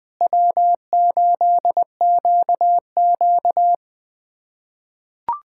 Morse code, heard as a keyed tone, about 20 wpm, 700 Hz. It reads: W8QQ
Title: USA Callsign DE Callsign - 4 Characters - 20wpm